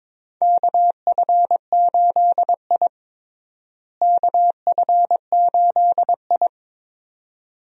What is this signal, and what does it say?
Morse code, heard as a keyed tone, about 22 wpm, 700 Hz. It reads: KF8I KF8I
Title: USA Call Signs - 4 Characters - Repeated 2x - 22wpm